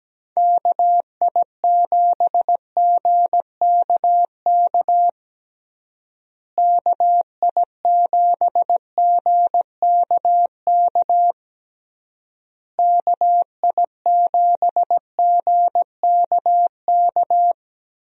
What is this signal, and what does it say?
Morse code, heard as a keyed tone, about 17 wpm, 700 Hz. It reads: KI7GKK KI7GKK KI7GKK
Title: USA Call Signs - 6 Characters - Repeated 3x - 17wpm